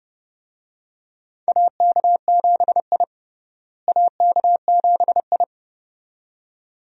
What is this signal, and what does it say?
Morse code, heard as a keyed tone, about 30 wpm, 700 Hz. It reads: AK7I AK7I
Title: USA Call Signs - 4 Characters - Repeated 2x - 30wpm